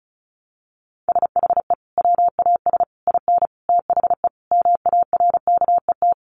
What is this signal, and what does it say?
Morse code, heard as a keyed tone, about 34 wpm, 700 Hz. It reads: SHE WAS IN THE MARKET
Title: Sentences from Top 600 Words 35wpm